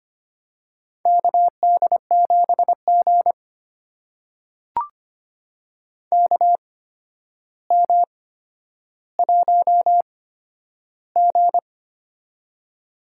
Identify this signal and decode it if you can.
Morse code, heard as a keyed tone, about 25 wpm, spaced as for 11 wpm, 700 Hz. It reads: KD7G KM1G
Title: USA Call Signs - 4 Characters - 8x Character Spacing - 25wpm